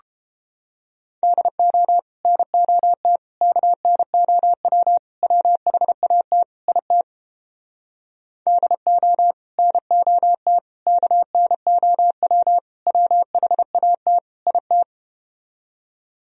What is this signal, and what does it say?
Morse code, heard as a keyed tone, about 32 wpm, 700 Hz. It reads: DO NOT KNOW WHAT IT DO NOT KNOW WHAT IT